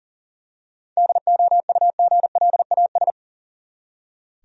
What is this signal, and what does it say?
Morse code, heard as a keyed tone, about 37 wpm, 700 Hz. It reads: DOUGLAS
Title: USA Common Names - 40wpm